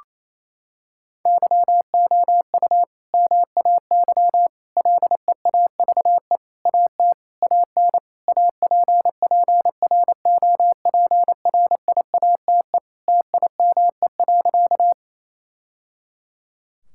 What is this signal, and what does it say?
Morse code, heard as a keyed tone, about 28 wpm, 700 Hz. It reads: YOU MAY LEAVE AT AN APPROPRIATE TIME.
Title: Sentences from Top 1000 Words - Review Part 5 - 28wpm